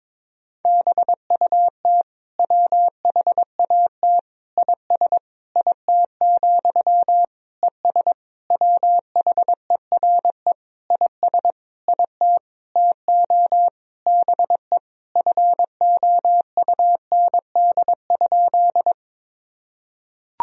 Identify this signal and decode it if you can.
Morse code, heard as a keyed tone, about 22 wpm, 700 Hz. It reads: BUT WHAT IS IT, ES WHERE IS IT TO BE FOUND?
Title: Sentences from Top 300 Words and Abbreviations 22wpm